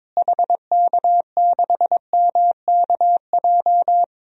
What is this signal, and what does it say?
Morse code, heard as a keyed tone, about 21 wpm, 700 Hz. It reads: HK6MKJ
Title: DXCC-100 Call Signs - 6 Characters - Repeated 3x - 22wpm